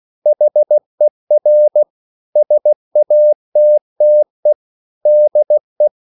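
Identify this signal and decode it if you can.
Morse code, heard as a keyed tone, about 16 wpm, 600 Hz. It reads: HER SATTE DE
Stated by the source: Slow (QRS) Morse code at 16 wpm, from CW med Gnister.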